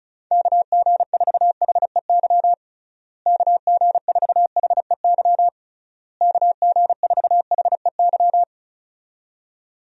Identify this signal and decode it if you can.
Morse code, heard as a keyed tone, about 36 wpm, 700 Hz. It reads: KG4HEY KG4HEY KG4HEY
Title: USA Call Signs - 6 Characters - Repeated 3x - 35wpm